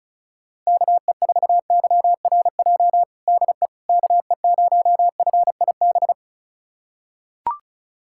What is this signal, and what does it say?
Morse code, heard as a keyed tone, about 34 wpm, 700 Hz. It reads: KE4YRJ DE KE0FIB
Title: USA Callsign DE Callsign - 6 Characters - 35wpm